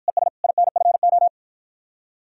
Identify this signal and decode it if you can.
Morse code, heard as a keyed tone, about 53 wpm, 700 Hz. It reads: ES INFO